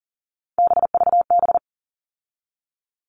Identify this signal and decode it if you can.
Morse code, heard as a keyed tone, about 40 wpm, 700 Hz. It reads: BVB
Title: All Cut Numbers - 3-Digit 40wpm